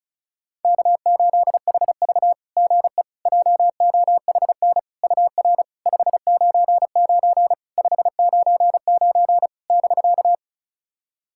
Morse code, heard as a keyed tone, about 36 wpm, 690 Hz. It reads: K8HV GE JOHN UR 599 599 <BK>
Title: SOTA QSO Practice 35wpm